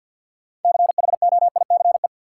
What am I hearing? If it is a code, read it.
Morse code, heard as a keyed tone, about 52 wpm, 700 Hz. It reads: CHOICE
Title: Sentences from Top 900 Words - New Word Drill - 50wpm